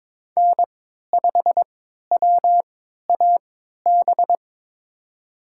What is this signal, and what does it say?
Morse code, heard as a keyed tone, about 22 wpm, spaced as for 12 wpm, 700 Hz. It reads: N5WAB
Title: USA Call Signs - 5 Characters - 3x Character Spacing - 22wpm